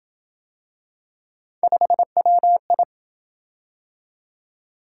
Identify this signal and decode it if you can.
Morse code, heard as a keyed tone, about 27 wpm, 700 Hz. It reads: 5WI